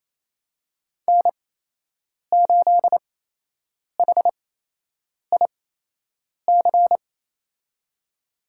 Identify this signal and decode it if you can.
Morse code, heard as a keyed tone, about 27 wpm, spaced as for 8 wpm, 700 Hz. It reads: N8HIC
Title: USA Call Signs - 5 Characters - 8x Character Spacing - 28wpm